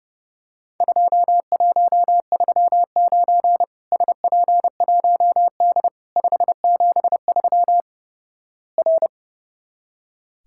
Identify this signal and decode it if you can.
Morse code, heard as a keyed tone, about 30 wpm, 700 Hz. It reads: 2139 SP1D 573 R
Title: Worked All Europe DX Contest - QTC Exchange - 30wpm